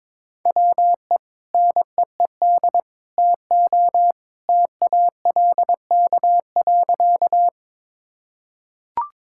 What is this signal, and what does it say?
Morse code, heard as a keyed tone, about 22 wpm, 700 Hz. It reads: WE NEED TO TALK.
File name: practice / 22wpm / Sentences with 2-Word Phrases - 22wpm